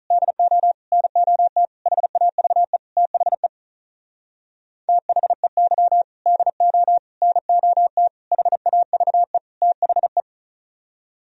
Morse code, heard as a keyed tone, about 45 wpm, 700 Hz. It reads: DO NOT HAVE THE THEY DO NOT HAVE THE